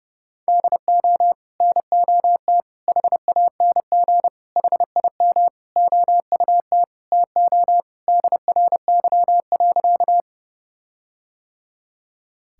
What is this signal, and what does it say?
Morse code, heard as a keyed tone, about 30 wpm, 700 Hz. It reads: DO NOT HANG HIM OUT TO DRY.
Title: Sentences from Top 1600 Words 30wpm